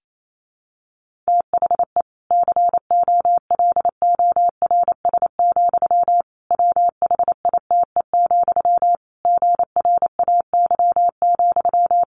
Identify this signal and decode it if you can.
Morse code, heard as a keyed tone, about 28 wpm, 700 Hz. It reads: THE COLORS, WHITE, GRAY,
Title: A Sound Of Thunder 28wpm